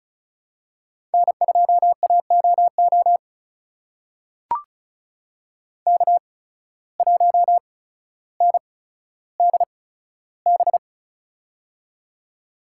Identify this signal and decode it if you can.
Morse code, heard as a keyed tone, about 36 wpm, spaced as for 15 wpm, 700 Hz. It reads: N2AOO K1NDB
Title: USA Call Signs - 5 Characters - 8x Character Spacing - 35wpm